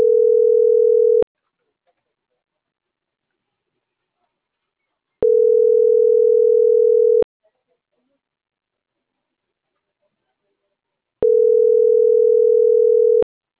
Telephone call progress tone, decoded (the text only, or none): ringback tone